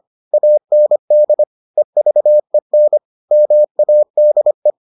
Morse code, heard as a keyed tone, about 25 wpm, 600 Hz. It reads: AND EVEN MADE